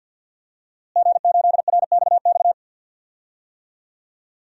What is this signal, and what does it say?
Morse code, heard as a keyed tone, about 47 wpm, 700 Hz. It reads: G8RXX
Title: DXCC-100 Call Signs - 5 Characters - Repeated 2x - 50wpm